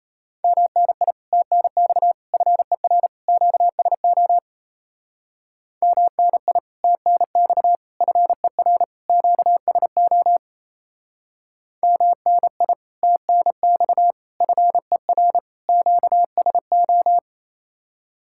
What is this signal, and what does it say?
Morse code, heard as a keyed tone, about 35 wpm, 700 Hz. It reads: MNI TNX FER QSO MNI TNX FER QSO MNI TNX FER QSO
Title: Speed-Racing Common QSO Phrases - 28wpm